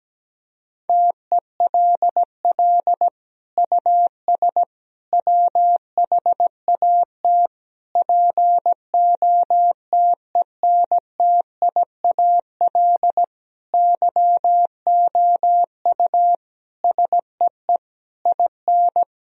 Morse code, heard as a keyed tone, about 17 wpm, 700 Hz. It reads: TELL US WHAT POTENTIAL YOU SEE IN
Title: Sentences from Top 1000 Words - Review Part 4 - 17wpm